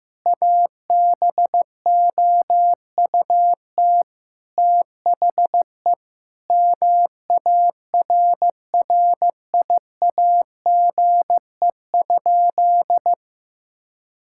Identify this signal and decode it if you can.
Morse code, heard as a keyed tone, about 15 wpm, 700 Hz. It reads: ABOUT THE MARRIAGE?
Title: Sentences from Top 1000 Words - Review Part 5 - 15wpm